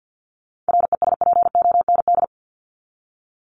Morse code, heard as a keyed tone, about 50 wpm, 700 Hz. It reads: RESPOND
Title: Top 1751-2000 Words 50wpm